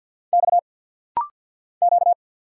Morse code, heard as a keyed tone, about 50 wpm, 700 Hz. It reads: X Q
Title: Single Letters - Rapid-Fire 50wpm